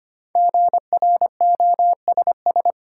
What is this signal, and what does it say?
Morse code, heard as a keyed tone, about 25 wpm, 700 Hz. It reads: GROSS
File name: 5 Letter Words - First 3 Letters Given - 25wpm